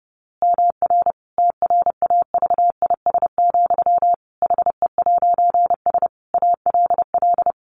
Morse code, heard as a keyed tone, about 30 wpm, 700 Hz. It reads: MR TRAVIS, HE'S ALL
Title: A Sound Of Thunder 30wpm